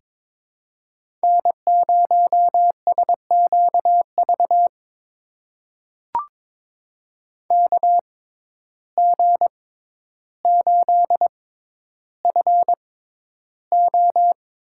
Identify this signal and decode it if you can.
Morse code, heard as a keyed tone, about 23 wpm, spaced as for 11 wpm, 700 Hz. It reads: N0SQV KG8FO
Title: USA Call Signs - 5 Characters - 6x Character Spacing - 22wpm